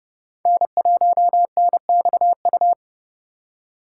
Morse code, heard as a keyed tone, about 37 wpm, 700 Hz. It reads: N1NXU
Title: Speed Racing - USA 1x3 Call Signs 30wpm